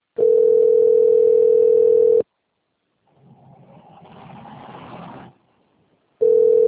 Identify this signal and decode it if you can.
Ringback tone (North America), 440 + 480 Hz.